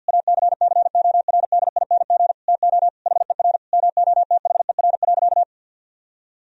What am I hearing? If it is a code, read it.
Morse code, heard as a keyed tone, about 50 wpm, 700 Hz. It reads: ACCORDING TO HER MOTHER.
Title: Sentences from Top 1000 Words - Review Part 4 - 50wpm